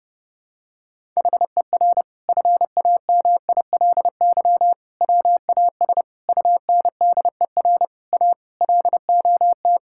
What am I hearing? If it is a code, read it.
Morse code, heard as a keyed tone, about 30 wpm, 700 Hz. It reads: HER FAMILY WAS UNDER A LOT
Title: Sentences from Top 1000 Words - Review Part 6 - 30wpm